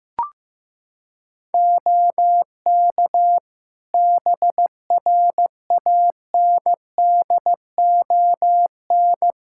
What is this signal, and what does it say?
Morse code, heard as a keyed tone, about 16 wpm, 700 Hz. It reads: OK BRANDON
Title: K1USN SST QSO Practice 15wpm